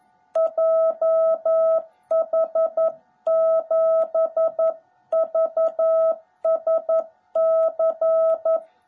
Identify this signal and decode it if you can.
Morse code, heard as a keyed tone, about 11 wpm, 660 Hz. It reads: JH7VSC
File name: jh8cbh / 2021cw / cw18b